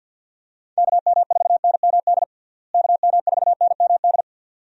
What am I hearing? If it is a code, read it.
Morse code, heard as a keyed tone, about 47 wpm, 700 Hz. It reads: KM4NMD KM4NMD
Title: USA Call Signs - 6 Characters - Repeated 2x - 50wpm